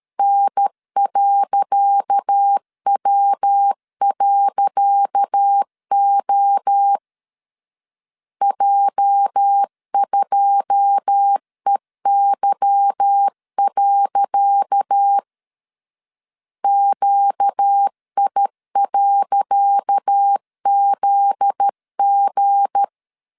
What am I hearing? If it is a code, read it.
Morse code, heard as a keyed tone, about 13 wpm, 800 Hz. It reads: N.W.O J2EY. QI.ZG
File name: LEZIONI / 10 SIGNS / 10C DOT 12wpm 2451